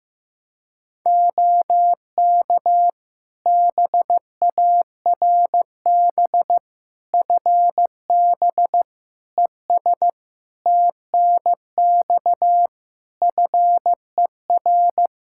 Morse code, heard as a keyed tone, about 15 wpm, 700 Hz. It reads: OK BARB FB ES TNX FER
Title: Practice QSO - Exchange 2 - Info - 15wpm